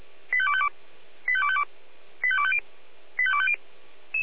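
Five-tone selective call (ZVEI), heard as quick runs of five tones, tones of about 70 ms. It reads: 85252 85252 85250 85250
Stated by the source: ZVEI five-tone selective call received on an SDR.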